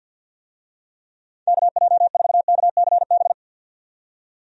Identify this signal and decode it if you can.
Morse code, heard as a keyed tone, about 52 wpm, 700 Hz. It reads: KJ4KCB
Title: USA Call Signs - 6 Characters - 50wpm